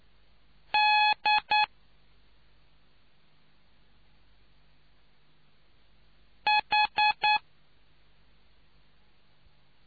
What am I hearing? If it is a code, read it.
Morse code, heard as a keyed tone, about 9 wpm, 840 Hz. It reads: D H